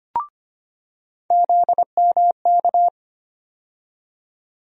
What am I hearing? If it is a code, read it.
Morse code, heard as a keyed tone, about 25 wpm, 700 Hz. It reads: ZMK
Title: USA Call Sign Postfixes - 3 Letters 25wpm